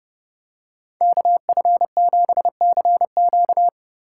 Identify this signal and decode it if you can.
Morse code, heard as a keyed tone, about 29 wpm, 700 Hz. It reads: KF7CQ